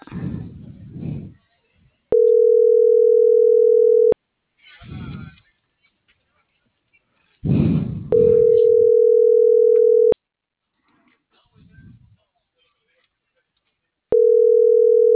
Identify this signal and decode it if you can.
Ringback tone (North America), 440 + 480 Hz.